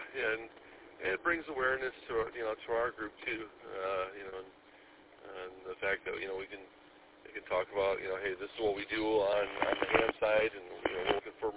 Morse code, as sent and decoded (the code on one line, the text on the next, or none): none